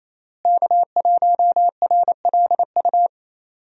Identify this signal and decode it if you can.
Morse code, heard as a keyed tone, about 28 wpm, 700 Hz. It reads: K1RLU
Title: USA Call Signs - 5 Characters - Repeated 2x - 28wpm